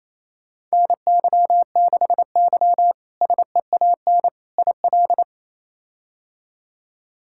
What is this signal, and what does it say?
Morse code, heard as a keyed tone, about 28 wpm, 700 Hz. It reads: NY6Y SEAN IL
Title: K1USN SST QSO Practice - 28wpm